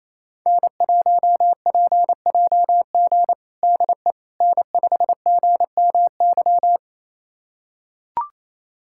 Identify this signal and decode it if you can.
Morse code, heard as a keyed tone, about 28 wpm, 700 Hz. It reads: N1PJG DE N5GMY
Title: USA Callsign DE Callsign - 5 Characters - 28wpm